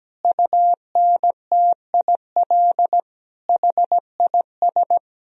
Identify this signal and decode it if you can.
Morse code, heard as a keyed tone, about 17 wpm, 700 Hz. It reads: UNTIL HIS
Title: Sentences from Top 1000 Words - Review Part 1 - 17wpm